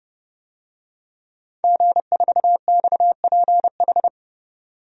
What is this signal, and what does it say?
Morse code, heard as a keyed tone, about 30 wpm, 700 Hz. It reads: G4XPH
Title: DXCC-100 Call Signs - 5 Characters - 30wpm